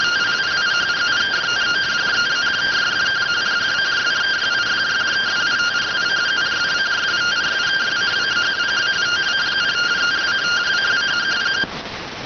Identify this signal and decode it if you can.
Two-tone teleprinter signal, about 170 Hz shift.